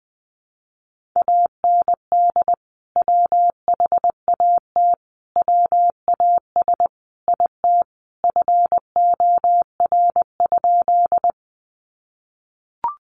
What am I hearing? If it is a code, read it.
Morse code, heard as a keyed tone, about 20 wpm, 700 Hz. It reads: AND WHAT WAS IT FOR?
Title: Sentences from Top 100 Words 20wpm